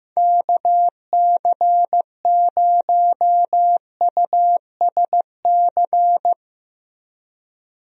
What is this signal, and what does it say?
Morse code, heard as a keyed tone, about 15 wpm, 700 Hz. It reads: KC0USC
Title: USA Call Signs - 6 Characters - Repeated 3x - 15wpm